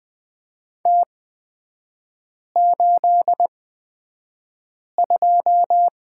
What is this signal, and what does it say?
Morse code, heard as a keyed tone, about 20 wpm, spaced as for 5 wpm, 700 Hz. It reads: T82
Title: 255 Lesson 41e - Period - Three Characters 20wpm